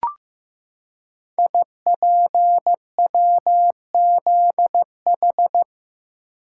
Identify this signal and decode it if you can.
Morse code, heard as a keyed tone, about 15 wpm, 700 Hz. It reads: IPWZH
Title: Code Groups - 5 Letters - 15wpm